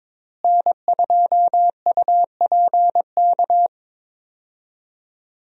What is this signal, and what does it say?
Morse code, heard as a keyed tone, about 22 wpm, 700 Hz. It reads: N2UPK
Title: USA Call Signs - 5 Characters - Repeated 3x - 22wpm